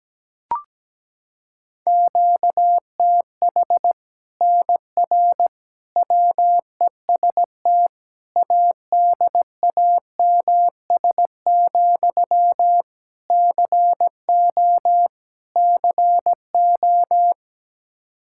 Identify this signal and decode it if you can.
Morse code, heard as a keyed tone, about 17 wpm, 700 Hz. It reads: QTH NR WEST ADAMS, CO CO
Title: Practice QSO - Exchange 1 - Beginning - 17wpm